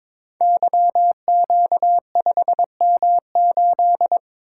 Morse code, heard as a keyed tone, about 23 wpm, 700 Hz. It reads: YQ5M8
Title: Code Groups - 5 Letter:Numbers - 22wpm